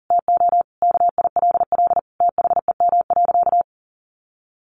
Morse code, heard as a keyed tone, about 40 wpm, 700 Hz. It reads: TO KILL THEM.
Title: Sentences from Top 1400 Words 40wpm